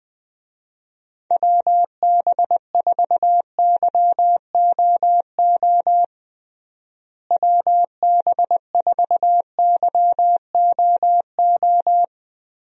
Morse code, heard as a keyed tone, about 21 wpm, 700 Hz. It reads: WB4YOO WB4YOO
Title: USA Call Signs - 6 Characters - Repeated 3x - 20wpm